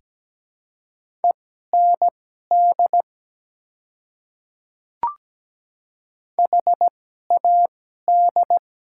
Morse code, heard as a keyed tone, about 17 wpm, spaced as for 12 wpm, 700 Hz. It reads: END HAD
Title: 2x Character Spacing - Top 200 Words - 17wpm